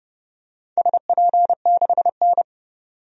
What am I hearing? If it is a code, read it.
Morse code, heard as a keyed tone, about 29 wpm, 700 Hz. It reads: SP6N
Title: DXCC-100 Call Signs - 4 Characters - 30wpm